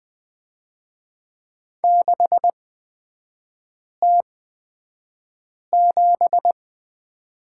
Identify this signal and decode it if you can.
Morse code, heard as a keyed tone, about 20 wpm, spaced as for 5 wpm, 700 Hz. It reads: 6T7
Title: 250 Lesson 40e - BK - Three Characters 20wpm